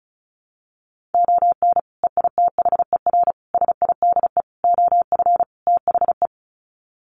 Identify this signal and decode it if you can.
Morse code, heard as a keyed tone, about 34 wpm, 700 Hz. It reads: ON EITHER SIDE OF THE
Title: Sets of 5 Words 35wpm